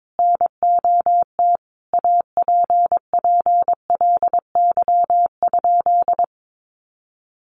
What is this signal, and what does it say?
Morse code, heard as a keyed tone, about 22 wpm, 700 Hz. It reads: NOT APPLY?
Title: Sentences from Top 1200 Words 22wpm